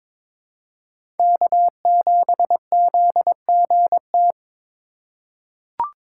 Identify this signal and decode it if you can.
Morse code, heard as a keyed tone, about 22 wpm, 700 Hz. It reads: K7ZGT
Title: USA Call Signs - 5 Characters - 4x Character Spacing - 22wpm